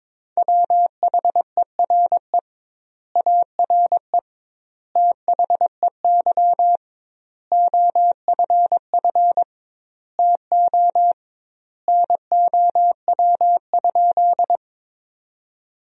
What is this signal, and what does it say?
Morse code, heard as a keyed tone, about 22 wpm, 700 Hz. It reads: WHERE ARE THEY OFF TO NOW?
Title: 2x Word-Spacing - Sentences from Top 200 Words - 22wpm